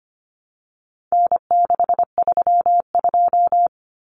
Morse code, heard as a keyed tone, about 25 wpm, 700 Hz. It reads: N632
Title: Common Cut Numbers - 4-Digit 25wpm